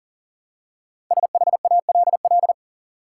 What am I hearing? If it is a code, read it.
Morse code, heard as a keyed tone, about 40 wpm, 700 Hz. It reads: SHALL
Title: Five Letter Words - 40wpm